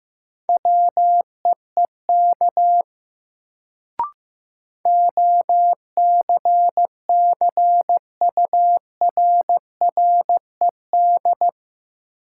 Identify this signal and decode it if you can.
Morse code, heard as a keyed tone, about 15 wpm, 700 Hz. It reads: WEEK OCCURRED
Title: Sets of 5 Words Encore - New Word Drill - 15wpm